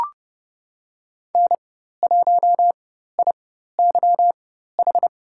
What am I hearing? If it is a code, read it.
Morse code, heard as a keyed tone, about 30 wpm, spaced as for 14 wpm, 700 Hz. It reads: N1IYH
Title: USA Call Signs - 5 Characters - 4x Character Spacing - 30wpm